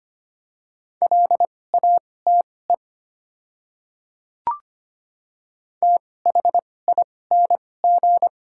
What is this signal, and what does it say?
Morse code, heard as a keyed tone, about 25 wpm, spaced as for 18 wpm, 700 Hz. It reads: LATE THING
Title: 2x Character Spacing - Top 300 Words - 25wpm